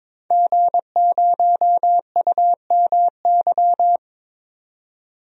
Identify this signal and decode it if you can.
Morse code, heard as a keyed tone, about 22 wpm, 700 Hz. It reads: G0UMY
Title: DXCC-100 Call Signs - 5 Characters - 22wpm